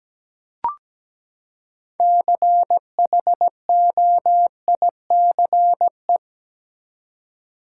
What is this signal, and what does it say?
Morse code, heard as a keyed tone, about 17 wpm, 700 Hz. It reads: CHOICE
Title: Six Letter Words - 17wpm